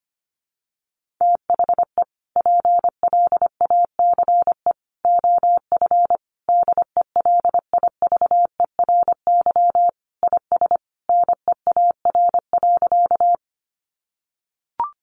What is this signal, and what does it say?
Morse code, heard as a keyed tone, about 25 wpm, 700 Hz. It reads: THE PLACE OF DELIVERY IS NEAR.
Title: Sentences from Top 1500 Words 25wpm